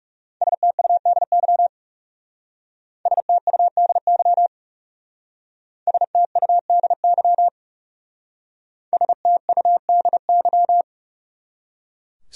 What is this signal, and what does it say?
Morse code, heard as a keyed tone, about 36 wpm, 700 Hz. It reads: STUDY STUDY STUDY STUDY
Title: Speed-Racing Top 500 Words 30wpm